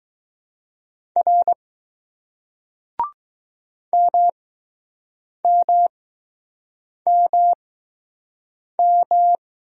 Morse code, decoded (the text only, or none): R MMMM